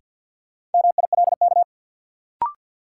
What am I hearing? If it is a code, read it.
Morse code, heard as a keyed tone, about 52 wpm, 700 Hz. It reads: MILK
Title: Sentences from Top 1500 Words - New Word Drill - 50wpm